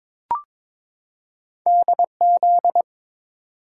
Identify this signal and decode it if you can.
Morse code, heard as a keyed tone, about 22 wpm, 700 Hz. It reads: DZ
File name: practice / 22wpm / Two Letter-Number Combinations - 22wpm